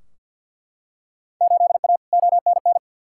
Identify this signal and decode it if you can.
Morse code, heard as a keyed tone, about 50 wpm, 700 Hz. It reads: 8A ONN
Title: Field Day Exchanges Encore - 50wpm